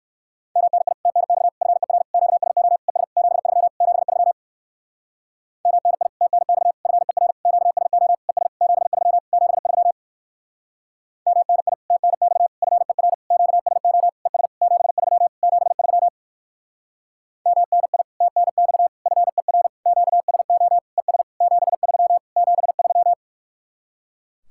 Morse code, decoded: MNI TNX FER QSO ES 73 73 MNI TNX FER QSO ES 73 73 MNI TNX FER QSO ES 73 73 MNI TNX FER QSO ES 73 73

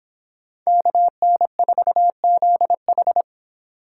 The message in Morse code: KN4ZH

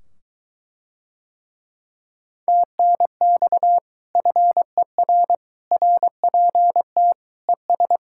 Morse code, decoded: TNX FER RPT ES

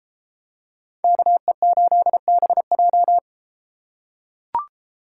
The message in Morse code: KE8BJ

